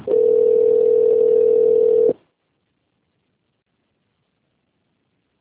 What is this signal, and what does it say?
Ringback tone (North America), 440 + 480 Hz.